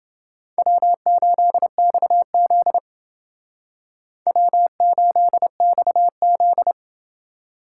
Morse code, heard as a keyed tone, about 28 wpm, 700 Hz. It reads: W8XZ W8XZ